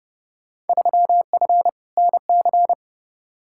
Morse code, heard as a keyed tone, about 29 wpm, 700 Hz. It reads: 3F NC
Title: Field Day Exchanges - 30wpm